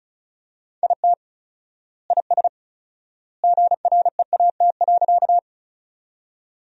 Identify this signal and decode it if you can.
Morse code, heard as a keyed tone, about 35 wpm, 700 Hz. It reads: IT IS GREAT.
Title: 4x Word-Spacing - Sentences from Top 200 Words - 35wpm